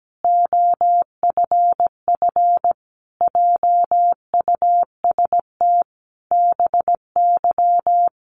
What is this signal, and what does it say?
Morse code, heard as a keyed tone, about 17 wpm, 700 Hz. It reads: OFF JUST BY
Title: Sentences from Top 800 Words 17wpm